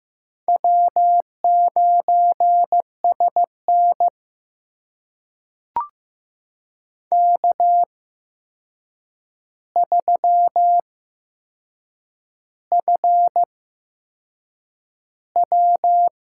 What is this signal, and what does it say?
Morse code, heard as a keyed tone, about 15 wpm, spaced as for 6 wpm, 700 Hz. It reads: W9SN K3FW
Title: USA Call Signs - 4 Characters - 8x Character Spacing - 15wpm